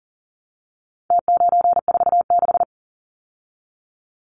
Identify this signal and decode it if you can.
Morse code, heard as a keyed tone, about 40 wpm, 700 Hz. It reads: T946
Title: Numbers - Contest Style - 4-Digit 40wpm